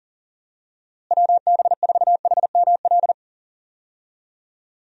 Morse code, heard as a keyed tone, about 40 wpm, 700 Hz. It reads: WB4HML
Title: USA Call Signs - 6 Characters - Repeated 2x - 40wpm